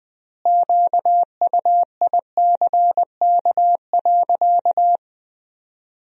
Morse code, heard as a keyed tone, about 20 wpm, 700 Hz. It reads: QUICK.
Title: 3x Word-Spacing - Sentences from Top 500 Words - 20wpm